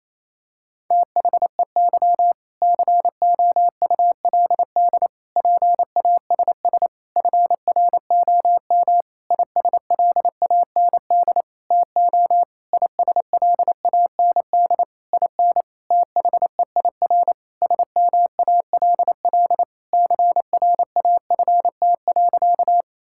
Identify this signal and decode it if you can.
Morse code, heard as a keyed tone, about 27 wpm, 700 Hz. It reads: THEY COULD PASS FROM ISLAND TO ISLAND IN THEIR SMALL CRAFT.
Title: Sentences from Top 1000 Words - Review Part 6 - 28wpm